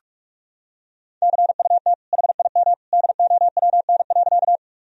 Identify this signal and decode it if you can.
Morse code, heard as a keyed tone, about 46 wpm, 700 Hz. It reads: CUT HIM DOWN.
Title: Sentences from Top 1000 Words - Review Part 1 - 45wpm